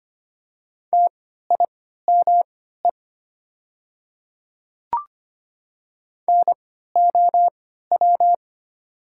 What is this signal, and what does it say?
Morse code, heard as a keyed tone, about 25 wpm, spaced as for 14 wpm, 700 Hz. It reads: TIME NOW
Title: 3x Character Spacing - Top 200 Words - 25wpm